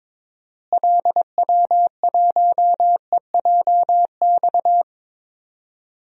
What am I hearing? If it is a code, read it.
Morse code, heard as a keyed tone, about 21 wpm, 700 Hz. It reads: LW1EJX
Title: DXCC-100 Call Signs - 6 Characters - Repeated 2x - 22wpm